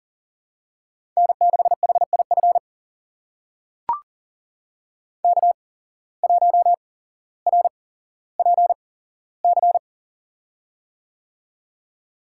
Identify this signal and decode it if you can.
Morse code, heard as a keyed tone, about 42 wpm, spaced as for 17 wpm, 700 Hz. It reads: N6HIF K1RPC
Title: USA Call Signs - 5 Characters - 8x Character Spacing - 40wpm